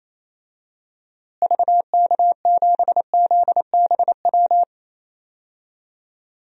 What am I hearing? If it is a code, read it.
Morse code, heard as a keyed tone, about 28 wpm, 700 Hz. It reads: VK7ZBW
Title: DXCC-100 Call Signs - 6 Characters - Repeated 2x - 28wpm